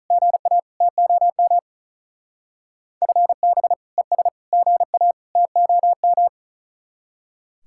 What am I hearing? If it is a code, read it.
Morse code, heard as a keyed tone, about 43 wpm, 700 Hz. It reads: GA TOM FB ES GA TOM
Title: Speed-Racing Common QSO Phrases - 35wpm